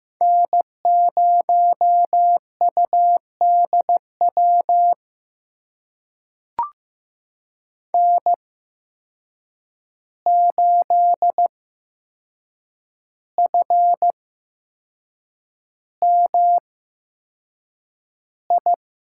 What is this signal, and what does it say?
Morse code, heard as a keyed tone, about 16 wpm, spaced as for 6 wpm, 700 Hz. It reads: N0UDW N8FMI
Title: USA Call Signs - 5 Characters - 8x Character Spacing - 15wpm